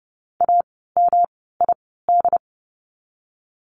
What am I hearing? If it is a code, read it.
Morse code, heard as a keyed tone, about 30 wpm, spaced as for 17 wpm, 700 Hz. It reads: AMID
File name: course / 30:10wpm / 108 Lesson 18f - Letter m - Words 30wpm